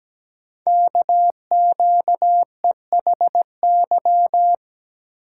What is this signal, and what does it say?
Morse code, heard as a keyed tone, about 17 wpm, 700 Hz. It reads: KQEHY
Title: Code Groups - 5 Letters - 17wpm